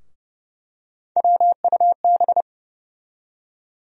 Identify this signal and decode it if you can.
Morse code, heard as a keyed tone, about 30 wpm, 700 Hz. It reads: WUB
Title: USA Call Sign Postfixes - 3 Letters 30wpm